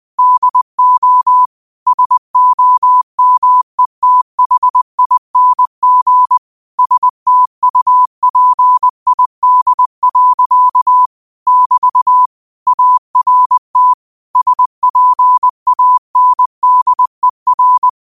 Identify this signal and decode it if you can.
Morse code, heard as a keyed tone, about 20 wpm, 1000 Hz. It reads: DO SOMETHING STUPID. <BT> ART SPANDER
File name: qotdfiles / 20wpm